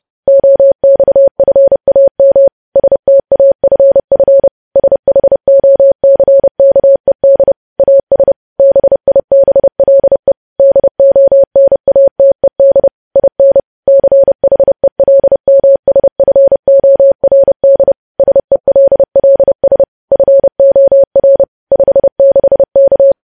Morse code, heard as a keyed tone, about 30 wpm, 550 Hz. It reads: OXFAM STAFF SHOCKED AS BIBLE DONATED IN CHELMSFORD SELLS FOR 56K
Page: news headlines in morse code at 30 wpm